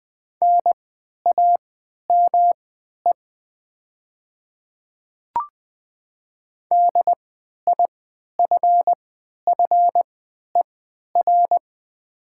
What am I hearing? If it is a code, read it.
Morse code, heard as a keyed tone, about 20 wpm, spaced as for 11 wpm, 700 Hz. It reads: NAME DIFFER
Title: 3x Character Spacing - Top 300 Words - 20wpm